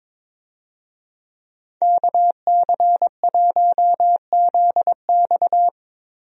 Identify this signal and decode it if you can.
Morse code, heard as a keyed tone, about 21 wpm, 700 Hz. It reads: KC1ZX
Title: USA Call Signs - 5 Characters - Repeated 2x - 22wpm